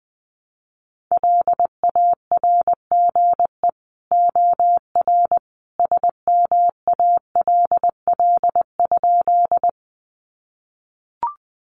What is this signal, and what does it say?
Morse code, heard as a keyed tone, about 20 wpm, 700 Hz. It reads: LARGE OR SMALL?
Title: Sentences from Top 200 Words 20wpm